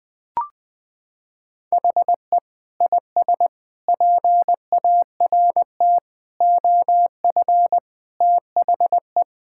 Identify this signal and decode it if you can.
Morse code, heard as a keyed tone, about 21 wpm, 700 Hz. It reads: HE IS PART OF THE
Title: Sentences with 3-Word Phrases - 20wpm